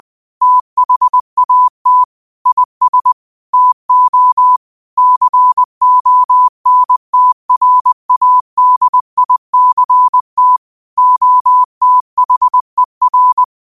Morse code, heard as a keyed tone, about 20 wpm, 1000 Hz. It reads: THAT IS TO CONTRADICT OTHER
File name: qotdfiles / 20wpm